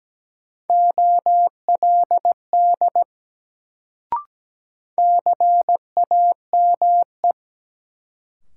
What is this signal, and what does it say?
Morse code, heard as a keyed tone, about 17 wpm, 700 Hz. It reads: OLD CAME
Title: Sentences from Top 200 Words - New Word Drill - 17wpm